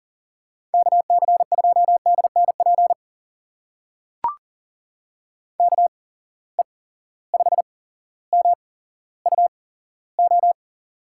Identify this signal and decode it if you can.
Morse code, heard as a keyed tone, about 39 wpm, spaced as for 17 wpm, 700 Hz. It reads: KC2DNP KE5MUO
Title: USA Call Signs - 6 Characters - 8x Character Spacing - 40wpm